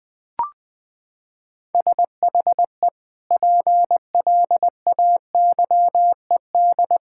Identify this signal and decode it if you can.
Morse code, heard as a keyed tone, about 20 wpm, 700 Hz. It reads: SHE PLAYED
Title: Sentences with 2-Word Phrases - 20wpm